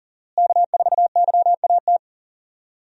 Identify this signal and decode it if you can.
Morse code, heard as a keyed tone, about 40 wpm, 700 Hz. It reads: K4YAT
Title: USA Call Signs - 5 Characters - Repeated 2x - 40wpm